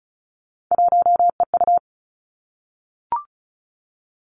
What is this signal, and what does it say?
Morse code, heard as a keyed tone, about 36 wpm, 700 Hz. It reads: 1EU
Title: Three Challenging Letter-Number Combinations 35wpm